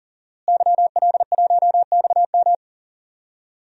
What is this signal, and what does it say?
Morse code, heard as a keyed tone, about 40 wpm, 700 Hz. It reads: YL1XM
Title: DXCC-100 Call Signs - 5 Characters - Repeated 2x - 40wpm